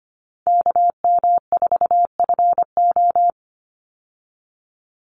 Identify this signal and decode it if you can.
Morse code, heard as a keyed tone, about 25 wpm, 700 Hz. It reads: KM4FO
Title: Call Signs Repeated 2x - 25wpm